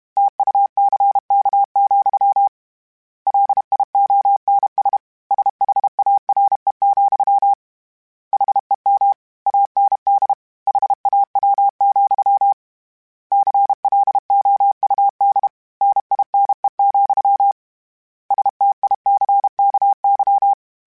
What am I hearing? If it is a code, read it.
Morse code, heard as a keyed tone, about 32 wpm, 800 Hz. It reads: TUCK, LIONS SHARE, HEM AND HAW, CLOUD NINE, STICKY
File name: Misc / Misc-mp3s / Word Pairs @30